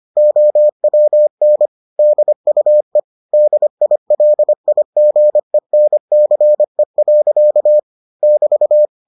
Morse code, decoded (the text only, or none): OWN DUE DILIGENCE. <BT>